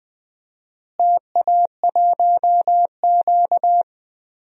Morse code, heard as a keyed tone, about 20 wpm, 700 Hz. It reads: TA1Q
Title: DXCC-100 Call Signs - 4 Characters - Repeated 2x - 20wpm